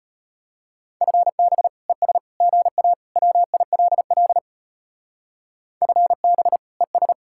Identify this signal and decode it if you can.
Morse code, heard as a keyed tone, about 32 wpm, 700 Hz. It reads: FB ES GA WILL FB ES